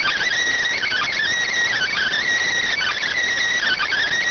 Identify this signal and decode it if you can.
Emergency Alert System (SAME) tones, heard as harsh buzzing bursts. It is heard from 0.7 to 3.2 s.